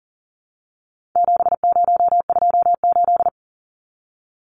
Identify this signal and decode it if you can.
Morse code, heard as a keyed tone, about 40 wpm, 700 Hz. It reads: 7028
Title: Numbers - 4-Digit 40wpm